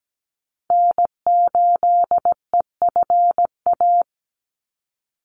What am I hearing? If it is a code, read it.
Morse code, heard as a keyed tone, about 17 wpm, 700 Hz. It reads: N8EFA
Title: USA Call Signs 17wpm